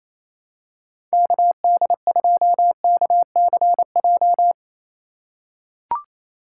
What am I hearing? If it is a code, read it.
Morse code, heard as a keyed tone, about 28 wpm, 700 Hz. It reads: KD2KCJ
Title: USA Call Signs - 4x Character Spacing - 28wpm